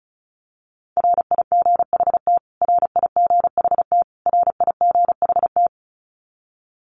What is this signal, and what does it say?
Morse code, heard as a keyed tone, about 34 wpm, 700 Hz. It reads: RIGHT RIGHT RIGHT
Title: Top 500 Words Repeated 35wpm